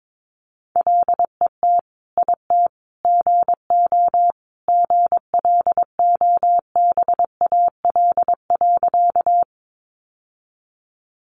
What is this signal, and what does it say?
Morse code, heard as a keyed tone, about 22 wpm, 700 Hz. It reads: LET IT GO GLOBAL.
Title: Sentences from Top 1200 Words 22wpm